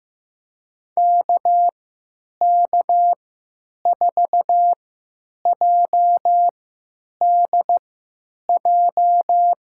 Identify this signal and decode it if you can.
Morse code, heard as a keyed tone, about 15 wpm, spaced as for 9 wpm, 700 Hz. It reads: KK4JDJ
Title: USA Call Signs - 3x Character Spacing - 15wpm